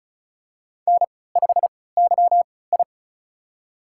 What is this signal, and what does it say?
Morse code, heard as a keyed tone, about 36 wpm, spaced as for 20 wpm, 700 Hz. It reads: N5YI
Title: USA Call Signs - 4 Characters - 3x Character Spacing - 35wpm